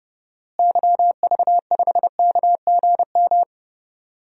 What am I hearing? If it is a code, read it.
Morse code, heard as a keyed tone, about 30 wpm, 700 Hz. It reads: YV5KGM
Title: DXCC-100 Call Signs - 6 Characters - Repeated 3x - 30wpm